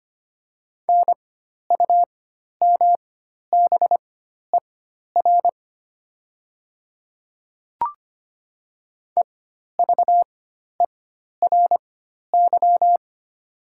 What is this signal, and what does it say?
Morse code, heard as a keyed tone, about 25 wpm, spaced as for 12 wpm, 700 Hz. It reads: NUMBER EVERY
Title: 4x Character Spacing - Top 300 Words - 25wpm